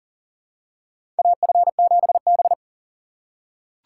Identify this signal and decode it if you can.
Morse code, heard as a keyed tone, about 40 wpm, 700 Hz. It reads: AF7B
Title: USA Call Signs - 4 Characters - 6x Character Spacing - 40wpm